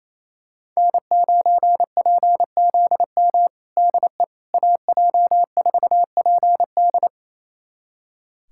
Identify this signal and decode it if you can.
Morse code, heard as a keyed tone, about 28 wpm, 700 Hz. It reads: N9PZM DE AJ4PD
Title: USA Callsign DE Callsign - 5 Characters - 28wpm